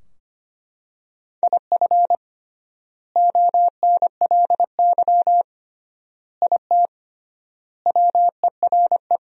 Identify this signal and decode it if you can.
Morse code, heard as a keyed tone, about 25 wpm, 700 Hz. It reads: IF ONLY IT WERE
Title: 3x Word-Spacing - Sentences from Top 500 Words - 25wpm